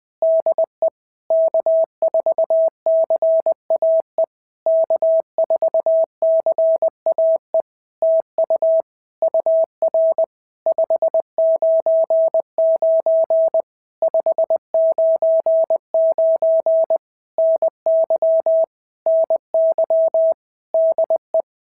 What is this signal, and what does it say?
Morse code, heard as a keyed tone, about 20 wpm, 650 Hz. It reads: DE K4CAE K4CAE TU UR 599 599 NY NY DE
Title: POTA QSO Practice - 20wpm